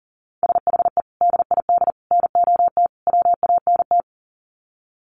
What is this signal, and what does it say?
Morse code, heard as a keyed tone, about 40 wpm, 700 Hz. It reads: SHE DID NOT WANT
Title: Sets of 4 Words 40wpm